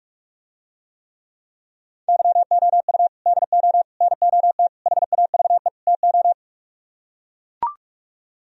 Morse code, heard as a keyed tone, about 44 wpm, 700 Hz. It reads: YOU DO NOT HAVE TO